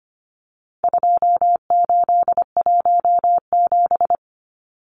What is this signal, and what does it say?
Morse code, heard as a keyed tone, about 25 wpm, 700 Hz. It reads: 2817
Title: Numbers - 4-Digit 25wpm